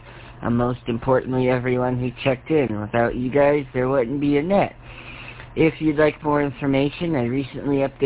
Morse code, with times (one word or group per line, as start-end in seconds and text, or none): none